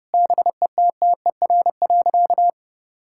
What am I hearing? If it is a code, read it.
Morse code, heard as a keyed tone, about 30 wpm, 700 Hz. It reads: BETTER.
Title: Sentences from Top 1000 Words - Review Part 6 - 30wpm